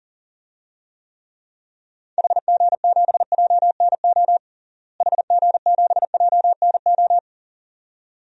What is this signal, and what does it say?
Morse code, heard as a keyed tone, about 37 wpm, 700 Hz. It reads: HG7JNO HG7JNO
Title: DXCC-100 Call Signs - 6 Characters - Repeated 2x - 40wpm